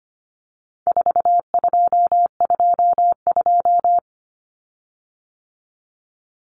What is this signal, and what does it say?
Morse code, heard as a keyed tone, about 25 wpm, 700 Hz. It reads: 4222
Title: Common Cut Numbers - 4-Digit 25wpm